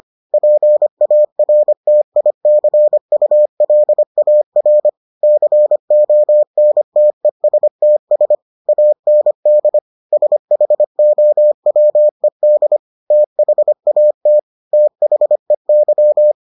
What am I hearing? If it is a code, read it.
Morse code, heard as a keyed tone, about 25 wpm, 600 Hz. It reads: PARTICULAR CONTESTS AND SHOWED THAT THEY